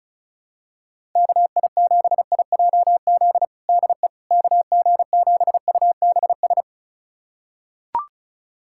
Morse code, heard as a keyed tone, about 36 wpm, 700 Hz. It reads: KI7IJZ DE KG7UBS
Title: USA Callsign DE Callsign 35wpm